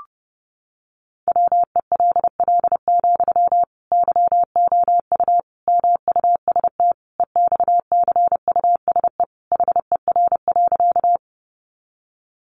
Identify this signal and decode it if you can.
Morse code, heard as a keyed tone, about 30 wpm, 700 Hz. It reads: WELL, YOU MUST EXCUSE HER.
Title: Sentences from Top 700 Words 30wpm